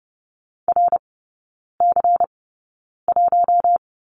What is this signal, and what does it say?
Morse code, heard as a keyed tone, about 28 wpm, spaced as for 9 wpm, 690 Hz. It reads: RC1
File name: Lesson 16e - Letter c - Three Characters 30wpm